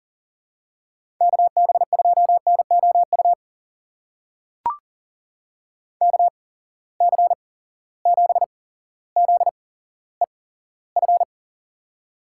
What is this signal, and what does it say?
Morse code, heard as a keyed tone, about 40 wpm, spaced as for 17 wpm, 700 Hz. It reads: KB2NOU KC7ZEF
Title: USA Call Signs - 8x Character Spacing - 40wpm